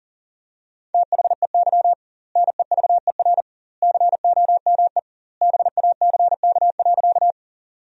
Morse code, heard as a keyed tone, about 40 wpm, 700 Hz. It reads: THEY NEVER COME BACK.
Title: 2x Word-Spacing - Sentences from Top 300 Words - 40wpm